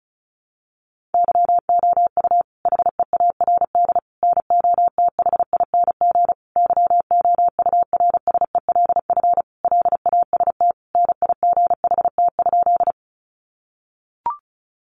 Morse code, heard as a keyed tone, about 34 wpm, 700 Hz. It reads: YOU HEARD NOTHING YOURSELF LAST NIGHT?
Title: Sentences from Top 600 Words 35wpm